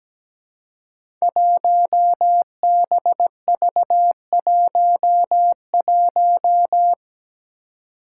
Morse code, heard as a keyed tone, about 17 wpm, 700 Hz. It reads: 1BV11
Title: Code Groups - 5 Letter:Numbers - 17wpm